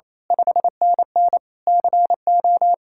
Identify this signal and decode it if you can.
Morse code, heard as a keyed tone, about 26 wpm, 700 Hz. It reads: 5NN CO
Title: Speed-Racing Common QSO Phrases - 28wpm